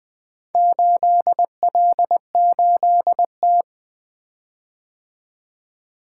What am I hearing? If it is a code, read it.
Morse code, heard as a keyed tone, about 20 wpm, 700 Hz. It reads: 8L8T
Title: Four Letter-Number Combinations - 20wpm